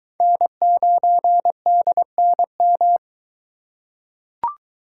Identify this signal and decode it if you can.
Morse code, heard as a keyed tone, about 21 wpm, 700 Hz. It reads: N9DNM